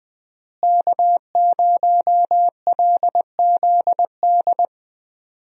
Morse code, heard as a keyed tone, about 20 wpm, 700 Hz. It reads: K0LZD
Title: USA Call Signs - 5 Characters - Repeated 3x - 20wpm